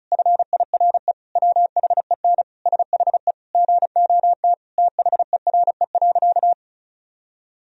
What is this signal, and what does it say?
Morse code, heard as a keyed tone, about 36 wpm, 700 Hz. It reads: FIRE WHEN SHE GOT THERE.
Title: Sentences from Top 1000 Words - Review Part 1 - 35wpm